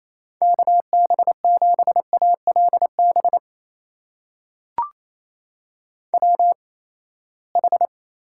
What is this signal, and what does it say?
Morse code, heard as a keyed tone, about 28 wpm, spaced as for 19 wpm, 700 Hz. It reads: KB7ALB WH
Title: USA Call Signs - 6 Characters - 8x Character Spacing - 28wpm